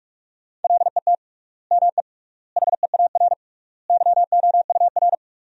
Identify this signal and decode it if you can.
Morse code, heard as a keyed tone, about 46 wpm, 700 Hz. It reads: LET ME HEAR YOUR
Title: 3x Word-Spacing - Sentences from Top 300 Words - 45wpm